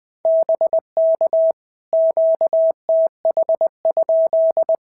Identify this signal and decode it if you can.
Morse code, heard as a keyed tone, about 19 wpm, 650 Hz. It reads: BK QTH?
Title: POTA QSO Practice - 20wpm